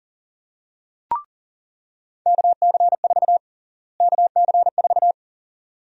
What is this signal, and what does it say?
Morse code, heard as a keyed tone, about 40 wpm, 700 Hz. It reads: KC4 KC4
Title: USA Call Sign Prefixes - Repeated 2x - 40wpm